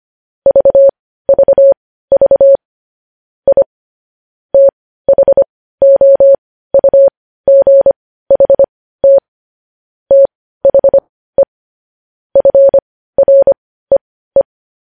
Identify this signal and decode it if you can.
Morse code, heard as a keyed tone, about 25 wpm, spaced as for 15 wpm, 550 Hz. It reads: VVV I THOUGHT THE FREE
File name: podcast / News headlines-1x-25x15